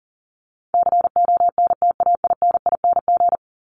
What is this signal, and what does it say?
Morse code, heard as a keyed tone, about 37 wpm, 700 Hz. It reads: CONTAINING
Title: ING Suffix - Top 100 Words with Base Words 40wpm